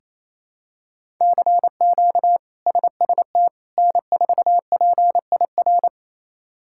Morse code, heard as a keyed tone, about 28 wpm, 700 Hz. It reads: CQ SST N4PIR
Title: K1USN SST QSO Practice - 28wpm